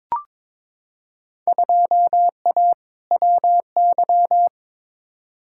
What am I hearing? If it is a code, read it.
Morse code, heard as a keyed tone, about 22 wpm, 700 Hz. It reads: 2A WY
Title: Field Day Exchanges - 22wpm